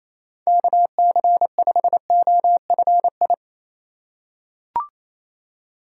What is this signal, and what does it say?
Morse code, heard as a keyed tone, about 28 wpm, 700 Hz. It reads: KC5OFI
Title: USA Call Signs - 6 Characters - 2x Character Spacing - 28wpm